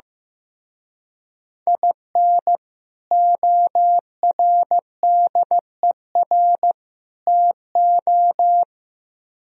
Morse code, heard as a keyed tone, about 15 wpm, 700 Hz. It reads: IN ORDER TO
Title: Speed-Racing - Sets of 3 Words 15wpm